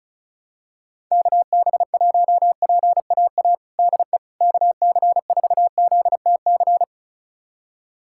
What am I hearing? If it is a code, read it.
Morse code, heard as a keyed tone, about 36 wpm, 700 Hz. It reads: KB1PAA DE KC4ZTC
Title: USA Callsign DE Callsign - 6 Characters - 35wpm